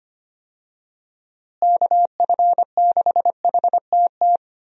Morse code, heard as a keyed tone, about 25 wpm, 700 Hz. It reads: KF6HTT
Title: USA Call Signs - 6 Characters - Repeated 3x - 25wpm